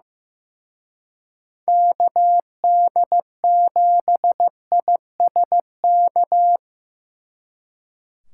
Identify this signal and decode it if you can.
Morse code, heard as a keyed tone, about 18 wpm, 700 Hz. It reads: KD7ISK
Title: Speed Racing - USA Call Signs 15wpm